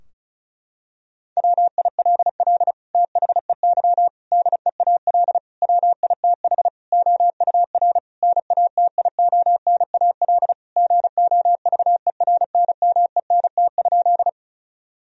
Morse code, heard as a keyed tone, about 34 wpm, 700 Hz. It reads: WILL THEY DEAL WITH OUR NATIONAL GOVERNMENT?
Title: Sentences from Top 1000 Words - Review Part 4 - 35wpm